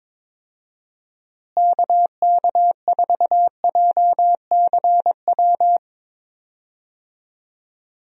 Morse code, heard as a keyed tone, about 22 wpm, 700 Hz. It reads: KK4JCW
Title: USA Call Signs - 6 Characters - Repeated 2x - 22wpm